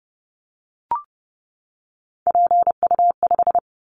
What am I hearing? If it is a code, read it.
Morse code, heard as a keyed tone, about 31 wpm, 700 Hz. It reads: PU5
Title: Callsign Prefixes for Top 50 DXCC - 30wpm